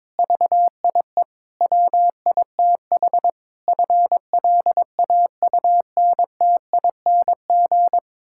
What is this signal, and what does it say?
Morse code, heard as a keyed tone, about 23 wpm, 700 Hz. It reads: VIE WITH FLAUNTING
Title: Pangram Challenge v2 - 22wpm